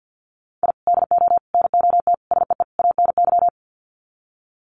Morse code, heard as a keyed tone, about 50 wpm, 700 Hz. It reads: I DO NOT SEE ANY
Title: Sets of 5 Words 50wpm